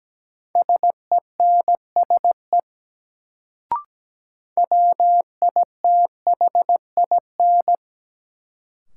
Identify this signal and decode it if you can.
Morse code, heard as a keyed tone, about 17 wpm, 700 Hz. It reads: SENSE WITHIN
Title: Sentences from Top 700 Words - New Word Drill - 17wpm